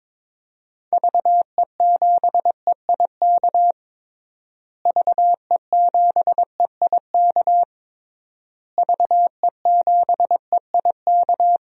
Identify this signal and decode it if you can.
Morse code, heard as a keyed tone, about 22 wpm, 700 Hz. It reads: VE7EIK VE7EIK VE7EIK
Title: DXCC-100 Call Signs - 6 Characters - Repeated 3x - 22wpm